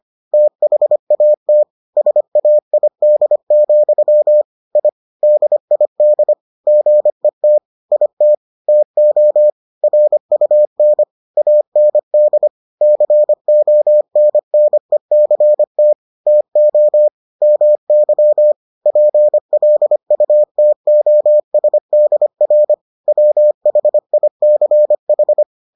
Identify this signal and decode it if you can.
Morse code, heard as a keyed tone, about 25 wpm, 600 Hz. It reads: THAT SAID, I DID GET IT TO RUN AND CONNECT TO MY PLUTOSDR WHICH